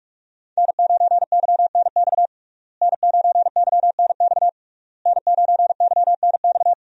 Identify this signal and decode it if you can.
Morse code, heard as a keyed tone, about 41 wpm, 700 Hz. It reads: N9YNX N9YNX N9YNX
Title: USA Call Signs - 5 Characters - Repeated 3x - 45wpm